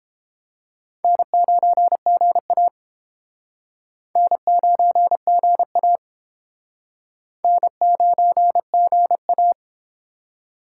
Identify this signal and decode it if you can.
Morse code, heard as a keyed tone, about 30 wpm, 700 Hz. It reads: N9GA N9GA N9GA